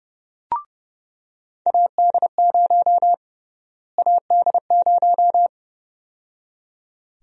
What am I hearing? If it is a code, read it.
Morse code, heard as a keyed tone, about 30 wpm, 700 Hz. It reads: AD0 AD0
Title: USA Call Sign Prefixes - Repeated 2x - 30wpm